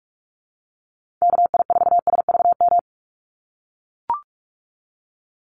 Morse code, heard as a keyed tone, about 44 wpm, 700 Hz. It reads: KI4SVM
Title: NA SOTA Activator Callsigns 45wpm